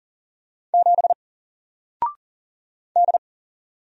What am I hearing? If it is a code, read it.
Morse code, heard as a keyed tone, about 40 wpm, 700 Hz. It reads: 7 D
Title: Single Letter-Number Alt 40wpm